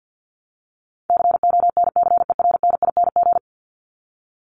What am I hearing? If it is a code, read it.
Morse code, heard as a keyed tone, about 49 wpm, 700 Hz. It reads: CONCERNING
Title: ING Suffix - Top 100 Words with Base Words 50wpm